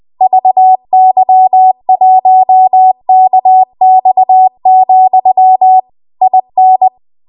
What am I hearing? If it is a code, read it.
Morse code, heard as a keyed tone, about 20 wpm, 750 Hz. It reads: VY1KX, IN
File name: Archive / 20 WPM / 20WPM